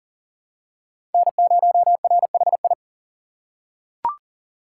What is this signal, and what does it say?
Morse code, heard as a keyed tone, about 40 wpm, 700 Hz. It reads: N0RHI
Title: USA Call Signs - 3x Character Spacing - 40wpm